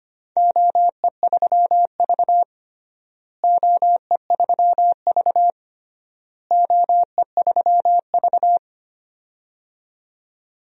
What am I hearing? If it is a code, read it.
Morse code, heard as a keyed tone, about 25 wpm, 700 Hz. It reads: OE3V OE3V OE3V
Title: DXCC-100 Call Signs - 4 Characters - Repeated 3x - 25wpm